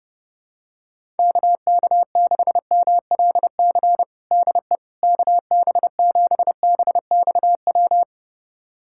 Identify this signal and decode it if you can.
Morse code, heard as a keyed tone, about 32 wpm, 700 Hz. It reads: KK6MLC DE KB7BXW
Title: USA Callsign DE Callsign 30wpm